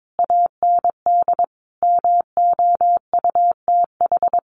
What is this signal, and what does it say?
Morse code, heard as a keyed tone, about 22 wpm, 700 Hz. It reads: AND MOUTH
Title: Sentences from Top 1500 Words 22wpm